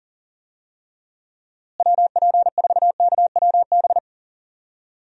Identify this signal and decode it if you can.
Morse code, heard as a keyed tone, about 40 wpm, 700 Hz. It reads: WP4KWB
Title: USA Call Signs - 6 Characters - Repeated 3x - 40wpm